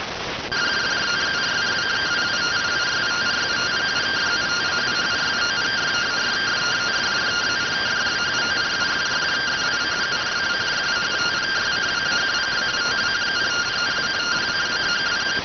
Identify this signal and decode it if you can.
Two-tone teleprinter signal, about 170 Hz shift.